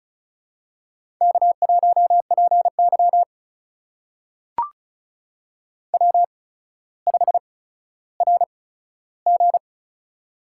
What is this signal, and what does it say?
Morse code, heard as a keyed tone, about 36 wpm, spaced as for 15 wpm, 700 Hz. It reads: K1PY W5RG
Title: USA Call Signs - 4 Characters - 8x Character Spacing - 35wpm